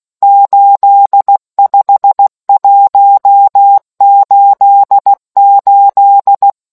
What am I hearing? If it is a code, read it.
Morse code, heard as a keyed tone, about 16 wpm, 800 Hz. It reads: 85188